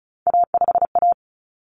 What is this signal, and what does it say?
Morse code, heard as a keyed tone, about 34 wpm, 700 Hz. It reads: A5A
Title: Common Cut Numbers - 3-Digit 35wpm